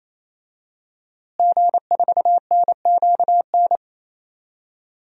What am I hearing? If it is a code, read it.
Morse code, heard as a keyed tone, about 27 wpm, 700 Hz. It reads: G4NQN
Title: DXCC-100 Call Signs - 5 Characters - Repeated 3x - 28wpm